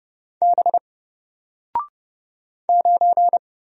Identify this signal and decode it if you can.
Morse code, heard as a keyed tone, about 30 wpm, 700 Hz. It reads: B 9
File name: Single-Letter-Number-30wpm